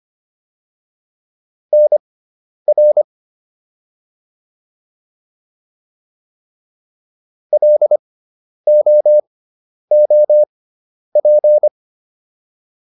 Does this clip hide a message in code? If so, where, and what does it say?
Yes: Morse code from 1.7 to 11.7 s, heard as a keyed tone, about 25 wpm, spaced as for 10 wpm, 600 Hz. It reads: NR LOOP